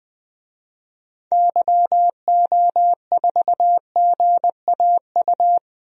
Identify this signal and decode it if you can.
Morse code, heard as a keyed tone, about 20 wpm, 700 Hz. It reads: YO4GAU
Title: DXCC-100 Call Signs - 6 Characters - Repeated 3x - 20wpm